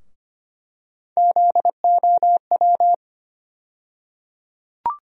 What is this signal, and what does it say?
Morse code, heard as a keyed tone, about 25 wpm, 700 Hz. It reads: ZOW
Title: USA Call Sign Postfixes - 3 Letters 25wpm